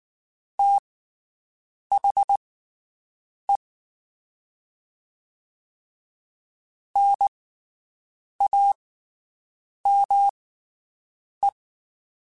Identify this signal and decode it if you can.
Morse code, heard as a keyed tone, about 19 wpm, spaced as for 7 wpm, 800 Hz. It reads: THE NAME
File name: (spaced) / QSOs Spaced-mp3s / WD0BGP 1 @18 sp